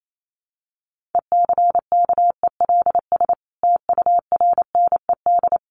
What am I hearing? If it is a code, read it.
Morse code, heard as a keyed tone, about 28 wpm, 700 Hz. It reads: ECKELS TURNED
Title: A Sound Of Thunder 28wpm